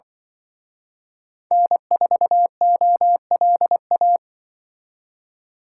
Morse code, heard as a keyed tone, about 23 wpm, 700 Hz. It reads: N4OLA
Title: Speed Racing - USA Call Signs 20wpm